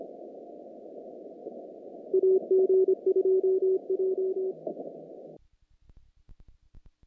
Morse code, heard as a keyed tone, about 25 wpm, 380 Hz. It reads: AG2J
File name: contests / naqp-cw-k5tr-op-k5pi- -1800